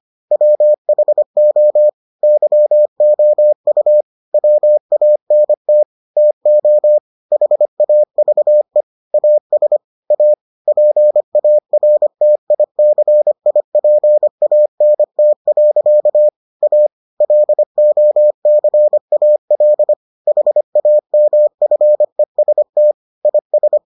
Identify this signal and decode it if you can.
Morse code, heard as a keyed tone, about 25 wpm, 600 Hz. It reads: WHO YOU WANT TO HAVE AS A PARTICIPANT. A LOCAL HAMFEST IS